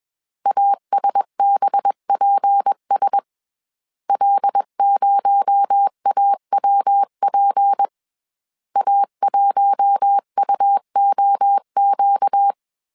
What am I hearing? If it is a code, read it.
Morse code, heard as a keyed tone, about 21 wpm, 800 Hz. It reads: ASBPS L0AWP A1UOQ
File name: CW / DET / 1 6 1500C S A 20wpm 2048